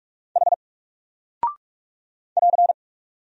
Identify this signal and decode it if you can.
Morse code, heard as a keyed tone, about 44 wpm, 700 Hz. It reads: H <AR>
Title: Single Character (Letter:Number:Punctuation:Prosign) 45wpm